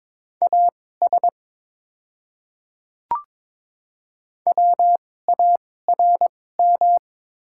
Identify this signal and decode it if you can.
Morse code, heard as a keyed tone, about 22 wpm, spaced as for 16 wpm, 700 Hz. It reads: AS WARM
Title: 2x Character Spacing - Top 500 Words - 22wpm